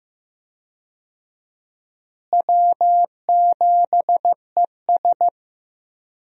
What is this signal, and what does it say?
Morse code, heard as a keyed tone, about 16 wpm, 700 Hz. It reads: W7ES
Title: USA Call Signs - 4 Characters - Repeated 2x - 15wpm